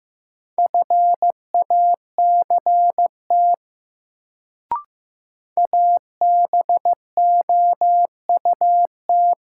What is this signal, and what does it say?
Morse code, heard as a keyed tone, about 15 wpm, 700 Hz. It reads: FACT ABOUT
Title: Sets of 2 Words - New Word Drill - 15wpm